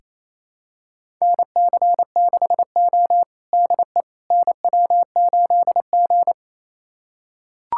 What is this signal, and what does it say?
Morse code, heard as a keyed tone, about 28 wpm, 690 Hz. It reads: NC6O DE NW8G
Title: USA Callsign DE Callsign - 4 Characters - 28wpm